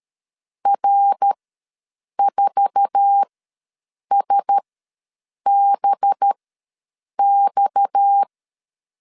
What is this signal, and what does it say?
Morse code, heard as a keyed tone, about 13 wpm, spaced as for 7 wpm, 800 Hz. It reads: R4SBX
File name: LEZIONI / 8 YJ45 / 8 YJ4 ws 12wpm 3249